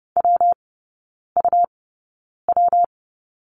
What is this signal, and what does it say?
Morse code, heard as a keyed tone, about 31 wpm, 700 Hz. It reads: W U W
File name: Lesson 19e - Letter w - Three Characters 30wpm